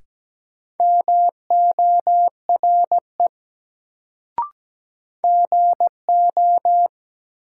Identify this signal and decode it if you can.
Morse code, heard as a keyed tone, about 17 wpm, 700 Hz. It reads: MORE GO
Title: Sentences from Top 100 Words - New Word Drill - 17wpm